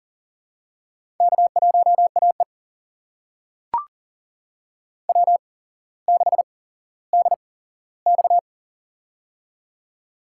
Morse code, heard as a keyed tone, about 40 wpm, spaced as for 17 wpm, 700 Hz. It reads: K1AE W6DX
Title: USA Call Signs - 4 Characters - 8x Character Spacing - 40wpm